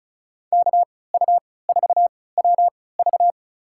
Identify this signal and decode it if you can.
Morse code, heard as a keyed tone, about 36 wpm, spaced as for 20 wpm, 700 Hz. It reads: KU4WV
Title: USA Call Signs - 5 Characters - 3x Character Spacing - 35wpm